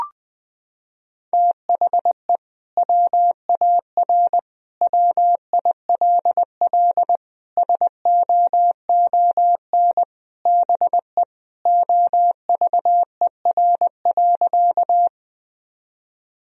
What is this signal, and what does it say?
Morse code, heard as a keyed tone, about 20 wpm, 700 Hz. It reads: THE WAR WILL SOON BE OVER.
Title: Sentences from Top 1000 Words - Review Part 2 - 20wpm